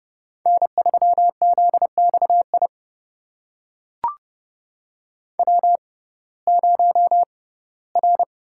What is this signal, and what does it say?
Morse code, heard as a keyed tone, about 30 wpm, spaced as for 18 wpm, 700 Hz. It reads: N3ZXI W0R
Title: USA Call Signs - 5 Characters - 6x Character Spacing - 30wpm